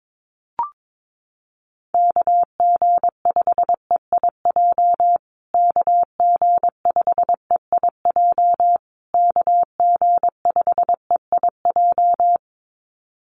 Morse code, heard as a keyed tone, about 22 wpm, 700 Hz. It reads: KG5EIJ KG5EIJ KG5EIJ
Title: USA Call Signs Repeated 3x 22wpm